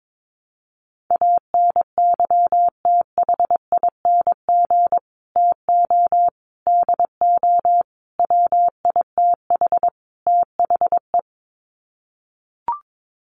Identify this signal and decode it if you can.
Morse code, heard as a keyed tone, about 22 wpm, 700 Hz. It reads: ANYTHING TO DO WITH THE
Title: Sets of 5 Words Encore 22wpm